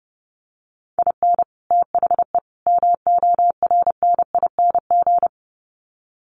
Morse code, heard as a keyed tone, about 30 wpm, 700 Hz. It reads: IN THE MORNING
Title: Sets of 3 Words 30wpm